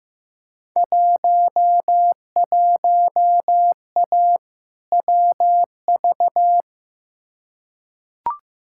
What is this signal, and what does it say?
Morse code, heard as a keyed tone, about 15 wpm, 700 Hz. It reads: 11A WV
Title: Field Day Exchanges Encore - 15wpm